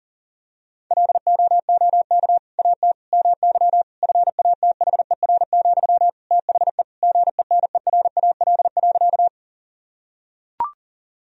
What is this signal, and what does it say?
Morse code, heard as a keyed tone, about 40 wpm, 700 Hz. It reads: LOOK AT MY FATHER, THE GENERAL.
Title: Sentences from Top 1000 Words - Review Part 3 - 40wpm